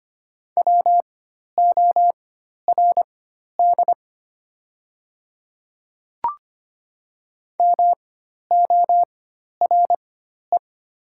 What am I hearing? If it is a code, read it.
Morse code, heard as a keyed tone, about 25 wpm, spaced as for 12 wpm, 690 Hz. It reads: WORD MORE
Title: 4x Character Spacing - Top 100 Words - 25wpm